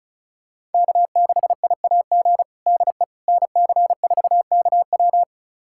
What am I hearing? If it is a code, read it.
Morse code, heard as a keyed tone, about 36 wpm, 700 Hz. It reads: K6IAG DE NC4KW
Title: USA Callsign DE Callsign - 5 Characters - 35wpm